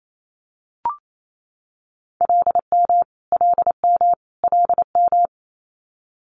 Morse code, heard as a keyed tone, about 28 wpm, 700 Hz. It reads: LM LM LM
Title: Multi-letter Phonemes Repeated 3x 28wpm